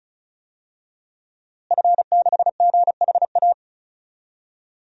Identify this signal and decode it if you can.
Morse code, heard as a keyed tone, about 35 wpm, 700 Hz. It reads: F6GHA
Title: DXCC-100 Call Signs - 5 Characters - Repeated 2x - 35wpm